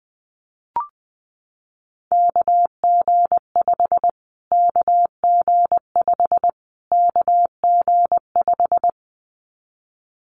Morse code, heard as a keyed tone, about 20 wpm, 700 Hz. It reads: KG5 KG5 KG5
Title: Callsign Prefixes for Top 25 DXCC - Repeated 3x - 20wpm